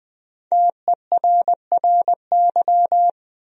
Morse code, heard as a keyed tone, about 21 wpm, 700 Hz. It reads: TERRY
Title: USA Common Names - Max 5 Letters - 20wpm